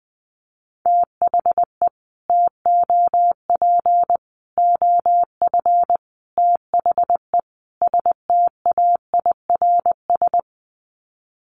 Morse code, heard as a keyed tone, about 20 wpm, 700 Hz. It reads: THE TOP OF THE STAIRS
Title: Sets of 5 Words Encore 20wpm